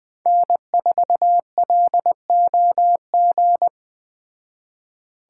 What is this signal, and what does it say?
Morse code, heard as a keyed tone, about 20 wpm, 700 Hz. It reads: N4LOG
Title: Speed Racing - USA 1x3 Call Signs 17wpm